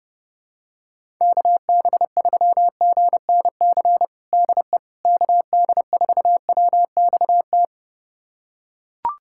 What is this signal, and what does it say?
Morse code, heard as a keyed tone, about 30 wpm, 700 Hz. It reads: KB3GNC DE KD4WXT
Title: USA Callsign DE Callsign - 6 Characters - 30wpm